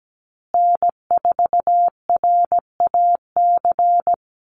Morse code, heard as a keyed tone, about 17 wpm, 700 Hz. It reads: N4RAC
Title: USA Call Signs Repeated 2x 17wpm